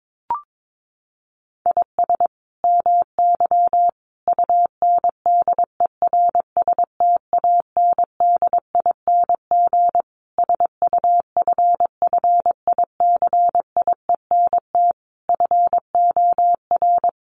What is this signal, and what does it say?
Morse code, heard as a keyed tone, about 22 wpm, 700 Hz. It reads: IS MY UNDERSTANDING SUFFICIENT FOR
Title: Sentences from Top 1400 Words 22wpm